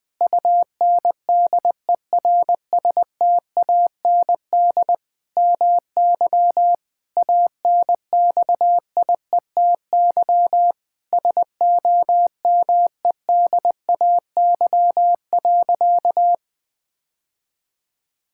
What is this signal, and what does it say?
Morse code, heard as a keyed tone, about 20 wpm, 700 Hz. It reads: UNDERSTAND MY ANXIETY SOMEDAY.
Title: Sentences from Top 1600 Words 20wpm